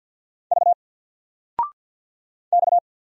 Morse code, heard as a keyed tone, about 50 wpm, 700 Hz. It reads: V X
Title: Single Letter-Number Alt 50wpm